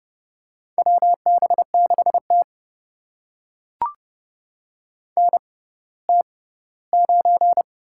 Morse code, heard as a keyed tone, about 30 wpm, spaced as for 17 wpm, 700 Hz. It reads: WB6T NT9
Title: USA Call Signs - 6x Character Spacing - 30wpm